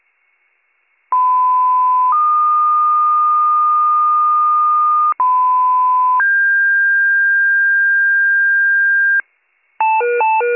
Two-tone page, heard as a one-second tone, then a three-second tone, about 1010 then 1220 Hz, about 1010 then 1600 Hz, with noise. It is heard from 1.1 to 9.2 s.